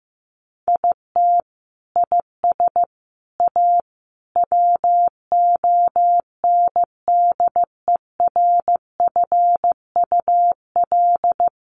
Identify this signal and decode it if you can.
Morse code, heard as a keyed tone, about 15 wpm, 700 Hz. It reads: IT IS A WONDERFUL
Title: Sentences from Top 1300 Words 15wpm